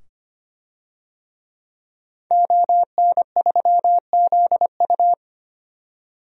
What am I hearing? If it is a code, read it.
Morse code, heard as a keyed tone, about 25 wpm, 700 Hz. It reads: ON3ZU
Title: DXCC-100 Call Signs - 5 Characters - 25wpm